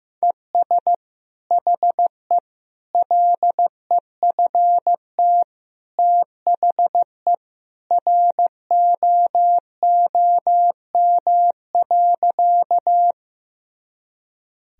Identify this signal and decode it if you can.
Morse code, heard as a keyed tone, about 15 wpm, 700 Hz. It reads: ES HE LEFT THE ROOM.
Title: Sentences from Top 300 Words and Abbreviations 15wpm